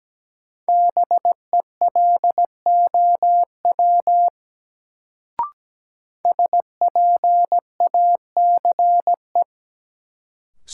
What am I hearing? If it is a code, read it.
Morse code, heard as a keyed tone, about 17 wpm, 700 Hz. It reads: BELOW SPACE
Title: Sentences from Top 1000 Words - Review Part 5 - New Word Drill - 17wpm